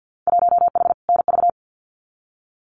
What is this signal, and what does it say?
Morse code, heard as a keyed tone, about 50 wpm, 700 Hz. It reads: 1H NV
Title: Winter Field Day Exchanges 50wpm